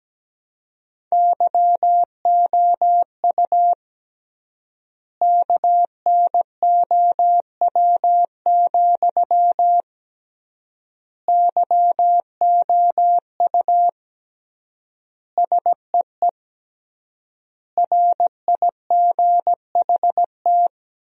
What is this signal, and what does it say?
Morse code, heard as a keyed tone, about 17 wpm, 700 Hz. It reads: YOU KNOW, YOU SEE RIGHT
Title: 3x Word-Spacing - Sentences from Top 200 Words - 17wpm